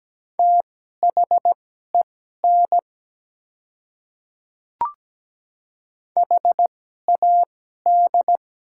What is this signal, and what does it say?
Morse code, heard as a keyed tone, about 17 wpm, spaced as for 12 wpm, 700 Hz. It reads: THEN HAD
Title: 2x Character Spacing - Top 100 Words - 17wpm